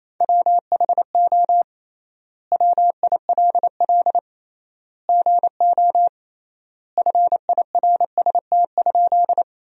Morse code, heard as a keyed tone, about 28 wpm, 700 Hz. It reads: WHO WILL GO FIRST?
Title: 3x Word-Spacing - Sentences from Top 100 Words - 28wpm